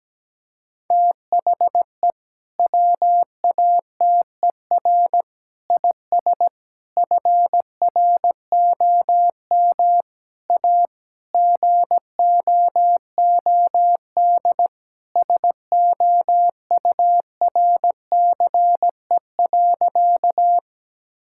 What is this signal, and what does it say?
Morse code, heard as a keyed tone, about 17 wpm, 700 Hz. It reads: THE WATER IS FROM A GOOD SOURCE.
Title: Sentences from Top 1000 Words - Review Part 5 - 17wpm